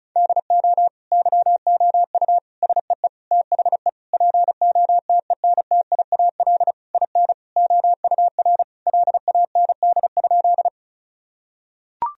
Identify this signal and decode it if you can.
Morse code, heard as a keyed tone, about 34 wpm, 700 Hz. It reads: DO YOU SEE THE POTENTIAL IN OUR LAND?
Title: Sentences from Top 1000 Words - Review Part 4 - 35wpm